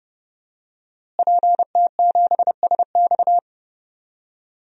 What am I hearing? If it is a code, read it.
Morse code, heard as a keyed tone, about 30 wpm, 700 Hz. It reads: PT7SX
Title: DXCC-100 Call Signs - 5 Characters - Repeated 2x - 30wpm